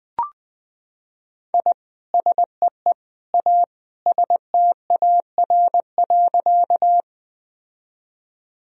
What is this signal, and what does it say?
Morse code, heard as a keyed tone, about 20 wpm, 700 Hz. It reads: I SEE A STAR.
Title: Sentences from Top 1000 Words - Review Part 2 - 20wpm